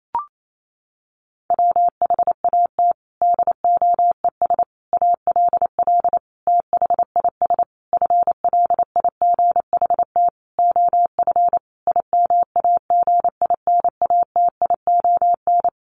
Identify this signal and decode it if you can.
Morse code, heard as a keyed tone, about 28 wpm, 700 Hz. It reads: WHAT DOES ALL THIS FLIGHT OF IMAGINATION
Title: Sentences from Top 1300 Words 28wpm